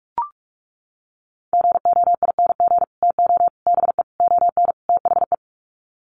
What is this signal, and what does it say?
Morse code, heard as a keyed tone, about 44 wpm, 690 Hz. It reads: GOING TO BE ON THE
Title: Sets of 5 Words Encore2 45wpm